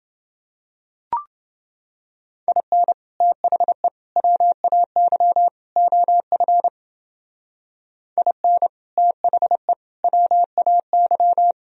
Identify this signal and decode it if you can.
Morse code, heard as a keyed tone, about 30 wpm, 700 Hz. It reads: IN THE WAY OF IN THE WAY